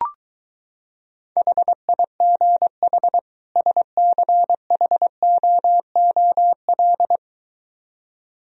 Morse code, heard as a keyed tone, about 23 wpm, 700 Hz. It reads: HIGH SCHOOL